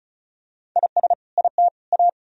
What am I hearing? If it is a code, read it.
Morse code, heard as a keyed tone, about 36 wpm, 700 Hz. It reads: IS IT A
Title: Sentences from Top 1000 Words - Review Part 2 - 35wpm